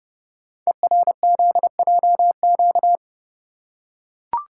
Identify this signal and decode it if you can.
Morse code, heard as a keyed tone, about 28 wpm, 690 Hz. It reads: ERZJQ
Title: Code Groups - 5 Letters - 30wpm